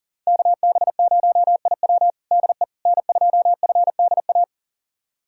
Morse code, heard as a keyed tone, about 37 wpm, 700 Hz. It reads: KB0IW DE N2FDA
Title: USA Callsign DE Callsign 40wpm